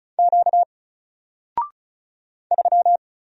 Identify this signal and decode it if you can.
Morse code, heard as a keyed tone, about 36 wpm, 700 Hz. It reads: Q 3
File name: Single-Letter-Number-35wpm